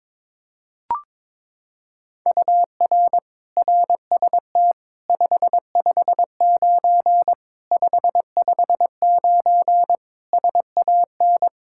Morse code, heard as a keyed tone, about 22 wpm, 700 Hz. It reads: UR RST 559 559 SAN
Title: Practice QSO - Exchange 1 - Beginning - 22wpm